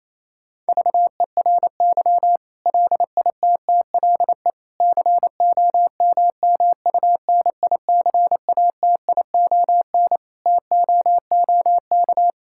Morse code, heard as a keyed tone, about 28 wpm, 700 Hz. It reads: VERY LITTLE COMMUNICATION TOOK
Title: Sentences from Top 1000 Words - Review Part 6 - 28wpm